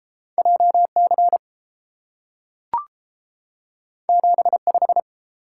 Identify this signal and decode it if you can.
Morse code, heard as a keyed tone, about 30 wpm, 700 Hz. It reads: JC 75